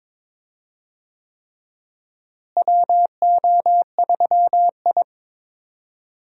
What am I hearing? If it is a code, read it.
Morse code, heard as a keyed tone, about 22 wpm, 700 Hz. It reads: WO3I
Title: USA Call Signs - 4 Characters - Repeated 3x - 22wpm